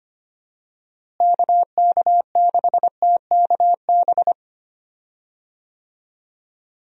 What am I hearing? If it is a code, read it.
Morse code, heard as a keyed tone, about 25 wpm, 700 Hz. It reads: KK6TKB
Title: USA Call Signs - 6 Characters - Repeated 3x - 25wpm